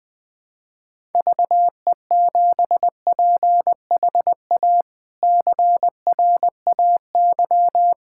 Morse code, heard as a keyed tone, about 20 wpm, 700 Hz. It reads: VE7PHA CRAY
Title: K1USN SST QSO Practice - 20wpm